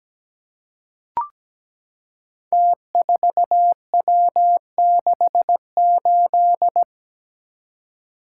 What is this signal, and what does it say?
Morse code, heard as a keyed tone, about 17 wpm, 700 Hz. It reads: T4W68
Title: Code Groups - 5 Letter:Numbers - 17wpm